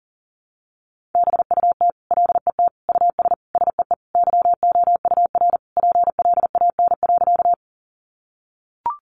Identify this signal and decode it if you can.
Morse code, heard as a keyed tone, about 40 wpm, 700 Hz. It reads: BUT LET US SEE YOUR PLAN.
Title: Sentences from Top 500 Words 40wpm